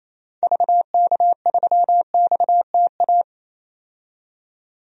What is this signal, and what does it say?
Morse code, heard as a keyed tone, about 28 wpm, 700 Hz. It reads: VK3XTA
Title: DXCC-100 Call Signs - 6 Characters - 28wpm